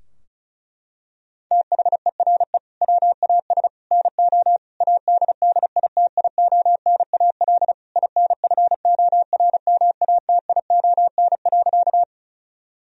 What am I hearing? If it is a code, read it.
Morse code, heard as a keyed tone, about 36 wpm, 700 Hz. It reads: THERE WAS NO ADDITIONAL INFORMATION.
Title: Sentences from Top 1000 Words - Review Part 4 - 35wpm